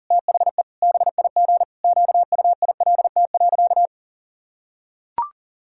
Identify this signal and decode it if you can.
Morse code, heard as a keyed tone, about 40 wpm, 700 Hz. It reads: THE BIG QUILT.
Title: Pangram Challenge v2 - 40wpm